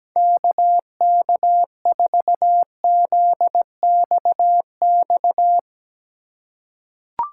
Morse code, heard as a keyed tone, about 17 wpm, 700 Hz. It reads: KK4ZXX
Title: USA Call Signs - 6 Characters - 4x Character Spacing - 17wpm